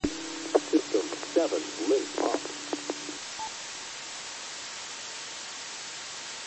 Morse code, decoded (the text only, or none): none